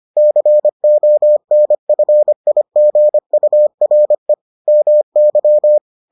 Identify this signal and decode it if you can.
Morse code, heard as a keyed tone, about 25 wpm, 600 Hz. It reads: CONFIGURE MY